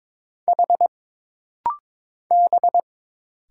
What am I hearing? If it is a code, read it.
Morse code, heard as a keyed tone, about 22 wpm, 700 Hz. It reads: H B
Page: This rapid-fire timing is geared to push your ICR (Instant Character Recognition) skills.